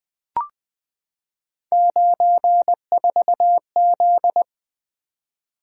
Morse code, heard as a keyed tone, about 20 wpm, 700 Hz. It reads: 94Z
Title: Three Letter-Number Combinations - 20wpm